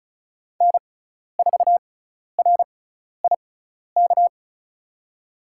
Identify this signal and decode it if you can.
Morse code, heard as a keyed tone, about 36 wpm, spaced as for 12 wpm, 700 Hz. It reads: N4RIK
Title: USA Call Signs - 6x Character Spacing - 35wpm